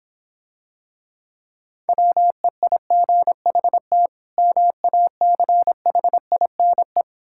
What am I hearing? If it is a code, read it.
Morse code, heard as a keyed tone, about 26 wpm, 700 Hz. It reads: WEIGHT MACHINE